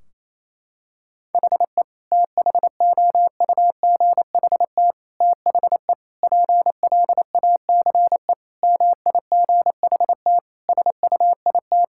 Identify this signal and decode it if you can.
Morse code, heard as a keyed tone, about 28 wpm, 700 Hz. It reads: HE THOUGHT THE PLACE MIGHT SUIT
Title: Sentences from Top 1600 Words 28wpm